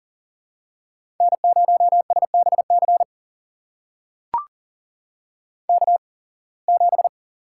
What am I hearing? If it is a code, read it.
Morse code, heard as a keyed tone, about 40 wpm, spaced as for 26 wpm, 700 Hz. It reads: N0SBC K7
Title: USA Call Signs - 8x Character Spacing - 40wpm